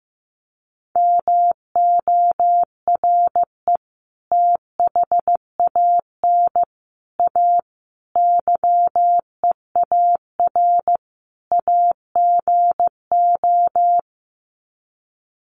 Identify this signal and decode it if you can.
Morse code, heard as a keyed tone, about 15 wpm, 700 Hz. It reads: MORE THAN A YEAR AGO
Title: Sets of 5 Words Encore2 15wpm